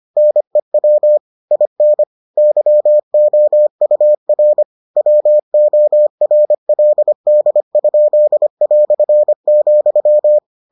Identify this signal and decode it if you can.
Morse code, heard as a keyed tone, about 25 wpm, 600 Hz. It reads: NEW IN YOUR WORLD?",